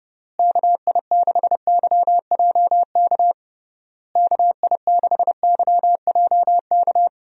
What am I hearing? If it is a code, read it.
Morse code, heard as a keyed tone, about 30 wpm, 700 Hz. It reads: KI6YJK KI6YJK
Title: USA Call Signs - 6 Characters - Repeated 3x - 30wpm